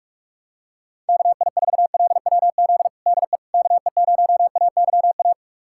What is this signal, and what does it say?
Morse code, heard as a keyed tone, about 41 wpm, 700 Hz. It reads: KI4LWZ DE KE0AYA
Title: USA Callsign DE Callsign - 6 Characters - 45wpm